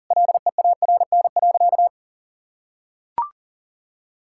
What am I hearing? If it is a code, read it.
Morse code, heard as a keyed tone, about 37 wpm, 700 Hz. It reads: LEARN.
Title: Sentences from Top 1000 Words - Review Part 6 - 40wpm